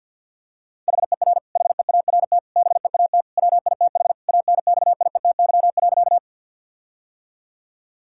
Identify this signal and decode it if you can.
Morse code, heard as a keyed tone, about 50 wpm, 700 Hz. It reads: HER HEART BEAT WITH ANXIETY.
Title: Sentences from Top 1600 Words 50wpm